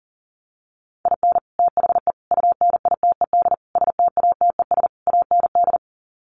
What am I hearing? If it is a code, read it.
Morse code, heard as a keyed tone, about 40 wpm, 700 Hz. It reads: IN THE UNITED STATES AND
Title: Sets of 5 Words 40wpm